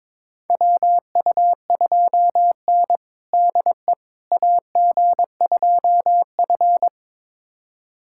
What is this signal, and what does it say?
Morse code, heard as a keyed tone, about 22 wpm, 700 Hz. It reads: WU2N DE AG2F
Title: USA Callsign DE Callsign - 4 Characters - 22wpm